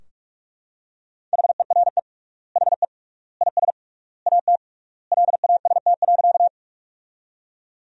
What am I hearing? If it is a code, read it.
Morse code, heard as a keyed tone, about 46 wpm, 700 Hz. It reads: HERE HE IS AT LAST.
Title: 3x Word-Spacing - Sentences from Top 300 Words - 45wpm